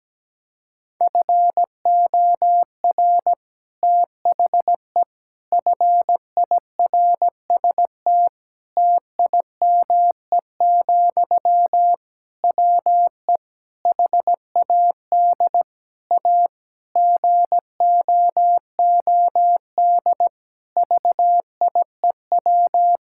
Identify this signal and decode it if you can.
Morse code, heard as a keyed tone, about 17 wpm, 700 Hz. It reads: FOR THE FIRST TIME, WE HAD A GOOD VIEW
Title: Sentences from Top 1000 Words - Review Part 6 - 17wpm